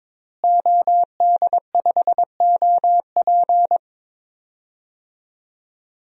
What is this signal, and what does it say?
Morse code, heard as a keyed tone, about 22 wpm, 700 Hz. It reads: OD5OP
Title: DXCC-100 Call Signs - 5 Characters - Repeated 2x - 22wpm